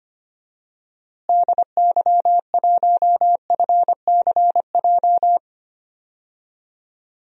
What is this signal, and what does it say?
Morse code, heard as a keyed tone, about 25 wpm, 700 Hz. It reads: DY1FCJ
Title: DXCC-100 Call Signs - 6 Characters - Repeated 3x - 25wpm